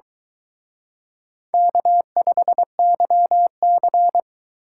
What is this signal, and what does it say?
Morse code, heard as a keyed tone, about 22 wpm, 700 Hz. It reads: K5YC